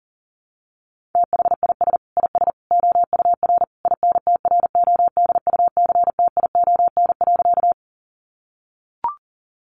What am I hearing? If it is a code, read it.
Morse code, heard as a keyed tone, about 40 wpm, 700 Hz. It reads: THIS IS OUR INTRODUCTION.
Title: Sentences from Top 1000 Words 40wpm